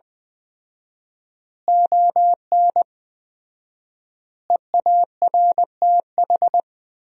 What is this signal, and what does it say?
Morse code, heard as a keyed tone, about 21 wpm, 700 Hz. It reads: ON EARTH
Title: 4x Word-Spacing - Sentences from Top 200 Words - 20wpm